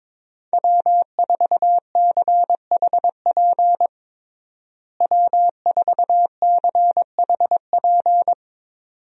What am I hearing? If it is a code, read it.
Morse code, heard as a keyed tone, about 21 wpm, 700 Hz. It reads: W4CHP W4CHP
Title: USA Call Signs - 5 Characters - Repeated 2x - 22wpm